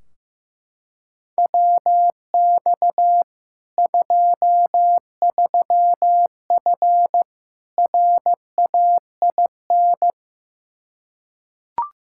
Morse code, heard as a keyed tone, about 15 wpm, 700 Hz. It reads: WX 23F RAIN
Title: Practice QSO - Exchange 2 - Info - 15wpm